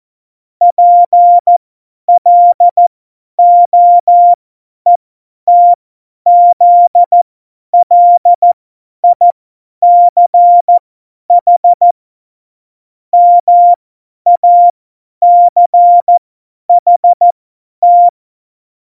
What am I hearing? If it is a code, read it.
Morse code, heard as a keyed tone, about 14 wpm, spaced as for 10 wpm, 700 Hz. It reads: PLOETZLICH MACHT